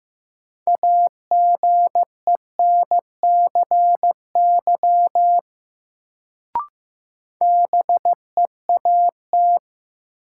Morse code, heard as a keyed tone, about 16 wpm, 700 Hz. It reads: AGENCY BEAT
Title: Sentences from Top 1300 Words - New Word Drill - 15wpm